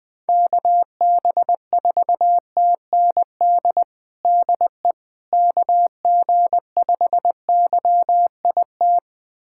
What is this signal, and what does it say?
Morse code, heard as a keyed tone, about 20 wpm, 700 Hz. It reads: KB4TND DE KG5YIT
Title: USA Callsign DE Callsign - 6 Characters - 20wpm